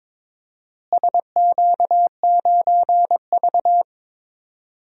Morse code, heard as a keyed tone, about 22 wpm, 700 Hz. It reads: SQ9V
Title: DXCC-100 Call Signs - 4 Characters - Repeated 2x - 22wpm